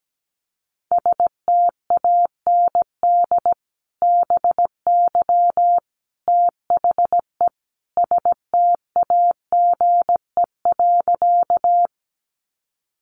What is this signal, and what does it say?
Morse code, heard as a keyed tone, about 17 wpm, 700 Hz. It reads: STAND BY THE STAGE.
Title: Sentences from Top 1200 Words 17wpm